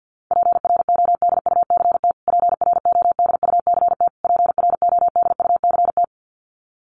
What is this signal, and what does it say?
Morse code, heard as a keyed tone, about 50 wpm, 700 Hz. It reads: PRODUCT PRODUCT PRODUCT
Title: Top 500 Words Repeated 50wpm